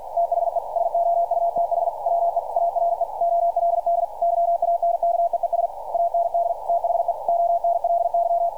Morse code, heard as a keyed tone, about 52 wpm, 690 Hz. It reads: THE SOUTH WE ARE OLD YANKEES AND THE ONLY
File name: archived / CRSnet